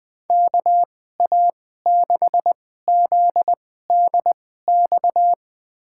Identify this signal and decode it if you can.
Morse code, heard as a keyed tone, about 20 wpm, spaced as for 14 wpm, 700 Hz. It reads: KA6ZDX
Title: USA Call Signs - 2x Character Spacing - 20wpm